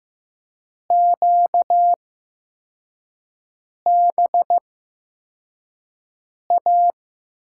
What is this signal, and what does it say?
Morse code, heard as a keyed tone, about 15 wpm, spaced as for 4 wpm, 700 Hz. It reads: QBA
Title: USA Call Signs - 8x Character Spacing - 15wpm